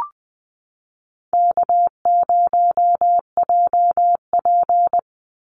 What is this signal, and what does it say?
Morse code, heard as a keyed tone, about 20 wpm, 700 Hz. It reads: K0JP
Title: Easy Call Signs - 20wpm